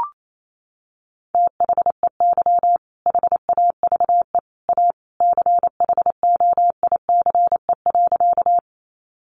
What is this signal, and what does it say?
Morse code, heard as a keyed tone, about 28 wpm, 700 Hz. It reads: THEY HAVE A CHOICE.
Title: Sentences from Top 900 Words 28wpm